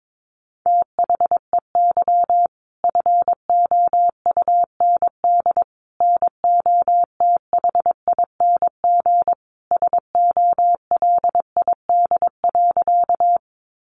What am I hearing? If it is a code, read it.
Morse code, heard as a keyed tone, about 22 wpm, 700 Hz. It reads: THEY FOUND NOTHING SOLID.
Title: Sentences from Top 1500 Words 22wpm